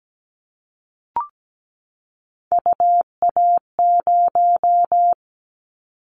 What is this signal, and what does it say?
Morse code, heard as a keyed tone, about 17 wpm, 700 Hz. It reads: UA0
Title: Callsign Prefixes for Top 100 DXCC - 17wpm